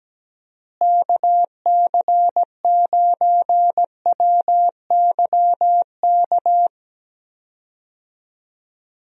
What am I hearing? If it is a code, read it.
Morse code, heard as a keyed tone, about 17 wpm, 700 Hz. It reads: KC9WYK
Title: USA Call Signs - 6 Characters - Repeated 2x - 17wpm